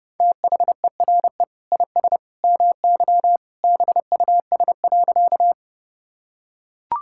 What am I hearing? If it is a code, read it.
Morse code, heard as a keyed tone, about 30 wpm, 700 Hz. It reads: THERE IS MY BUS.
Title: Sentences from Top 1000 Words - Review Part 6 - 30wpm